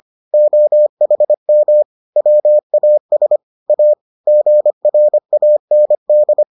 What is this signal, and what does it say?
Morse code, heard as a keyed tone, about 25 wpm, 600 Hz. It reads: OHM WAS A GRAND